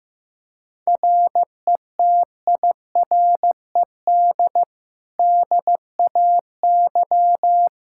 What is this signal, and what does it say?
Morse code, heard as a keyed tone, about 15 wpm, 700 Hz. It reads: RETIRED DAY
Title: Practice QSO - Exchange 3 - More Info - 15wpm